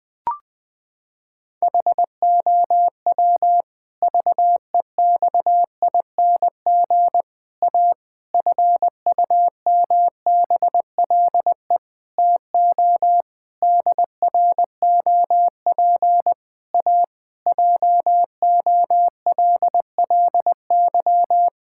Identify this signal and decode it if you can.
Morse code, heard as a keyed tone, about 19 wpm, 700 Hz. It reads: HOW VEXING A FUMBLE TO DROP A JOLLY
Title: Pangram Challenge v2 - 20wpm